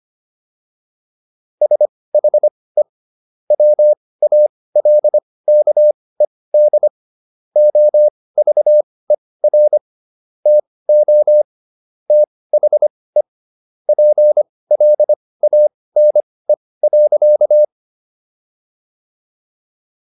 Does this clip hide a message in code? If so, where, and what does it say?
Yes: Morse code from 1.6 to 17.6 s, heard as a keyed tone, about 25 wpm, spaced as for 18 wpm, 600 Hz. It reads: SHE WALKED OVER TO THE PLANE.